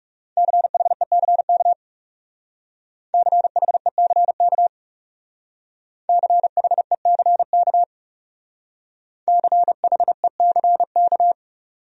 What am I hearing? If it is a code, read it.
Morse code, heard as a keyed tone, about 39 wpm, 700 Hz. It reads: CHECK CHECK CHECK CHECK